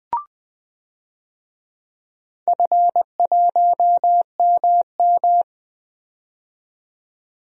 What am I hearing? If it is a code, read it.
Morse code, heard as a keyed tone, about 20 wpm, 700 Hz. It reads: F1MM
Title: DXCC-100 Call Signs - 4 Characters - 20wpm